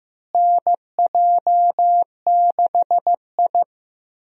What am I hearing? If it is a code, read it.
Morse code, heard as a keyed tone, about 14 wpm, 700 Hz. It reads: NJ6I
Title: USA Call Signs - 4 Characters - Repeated 2x - 15wpm